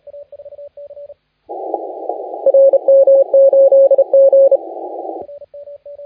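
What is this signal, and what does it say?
Morse code, heard as a keyed tone, about 30 wpm, 550 Hz, with radio noise. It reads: RM8G
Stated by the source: Morse contest audio recorded by LA4C during the SAC CW 2018 contest.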